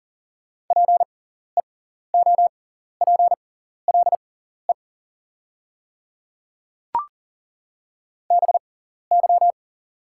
Morse code, heard as a keyed tone, about 40 wpm, spaced as for 14 wpm, 690 Hz. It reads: PEOPLE BY
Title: 6x Character Spacing - Top 100 Words - 40wpm